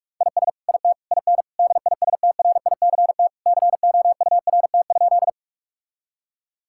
Morse code, heard as a keyed tone, about 43 wpm, 700 Hz. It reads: IS IT IN DISTRICT COURT?
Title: Sentences from Top 1600 Words 45wpm